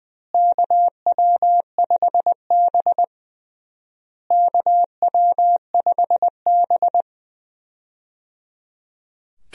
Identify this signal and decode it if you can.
Morse code, heard as a keyed tone, about 21 wpm, 700 Hz. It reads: KW5B KW5B
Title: USA Call Signs - 4 Characters - Repeated 3x - 20wpm